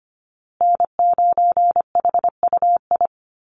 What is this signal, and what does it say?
Morse code, heard as a keyed tone, about 25 wpm, 700 Hz. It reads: N9HUI
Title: USA Call Signs Repeated 3x 25wpm